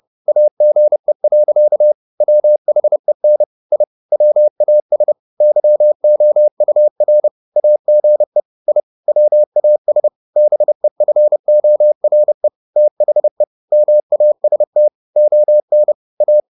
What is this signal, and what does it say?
Morse code, heard as a keyed tone, about 30 wpm, 600 Hz. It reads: AGE. WHEN I WAS YOUR AGE I WAS BEFORE THE MAST ON A
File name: the old man and the sea / the old man and the sea 1 30wpm